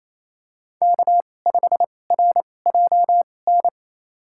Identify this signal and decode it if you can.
Morse code, heard as a keyed tone, about 28 wpm, spaced as for 20 wpm, 700 Hz. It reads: K5RJN
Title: USA Call Signs - 5 Characters - 2x Character Spacing - 28wpm